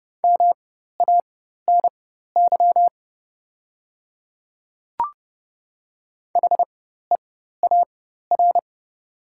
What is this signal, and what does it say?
Morse code, heard as a keyed tone, about 30 wpm, spaced as for 14 wpm, 700 Hz. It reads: MANY HEAR
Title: 4x Character Spacing - Top 300 Words - 30wpm